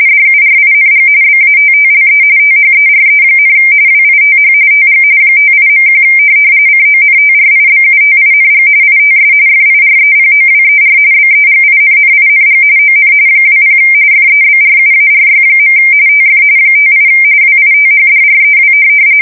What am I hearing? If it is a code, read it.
RTTY, about 45.45 baud. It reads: DURING BATTLE THERE, WITH TG 58.1 CONTRIBUTING ITS FIREPOWER BEGINNING ON 20 FEBRUARY. TF 58 WITHDREW
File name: fsk170 / USS-Massachusetts